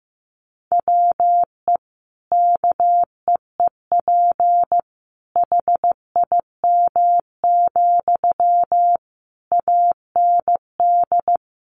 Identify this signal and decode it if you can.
Morse code, heard as a keyed tone, about 14 wpm, 700 Hz. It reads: WE KEEP HIM, AND
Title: Sentences from Top 600 Words 15wpm